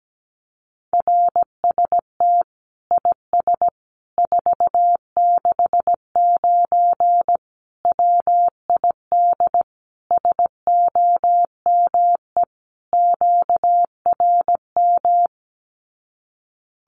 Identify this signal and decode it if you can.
Morse code, heard as a keyed tone, about 17 wpm, 700 Hz. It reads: RST IS 469 WID SOME QRM
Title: Common QSO Phrases 17wpm